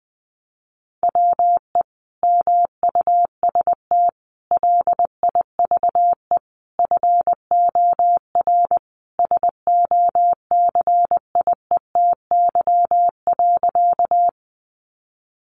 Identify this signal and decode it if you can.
Morse code, heard as a keyed tone, about 20 wpm, 700 Hz. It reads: WE MUST LIVE FOR SOCIETY.
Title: Sentences from Top 700 Words 20wpm